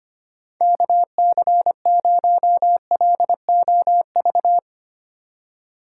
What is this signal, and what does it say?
Morse code, heard as a keyed tone, about 25 wpm, 700 Hz. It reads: KC0LOV
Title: USA Call Signs - 6 Characters - 2x Character Spacing - 25wpm